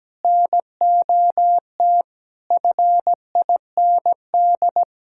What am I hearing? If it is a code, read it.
Morse code, heard as a keyed tone, about 17 wpm, 700 Hz. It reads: NOT FIND
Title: Sentences from Top 1000 Words - Review Part 5 - 17wpm